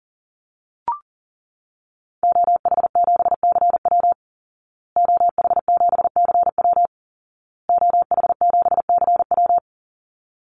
Morse code, heard as a keyed tone, about 40 wpm, 700 Hz. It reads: OH7CW OH7CW OH7CW
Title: Call Signs Repeated 2x - 40wpm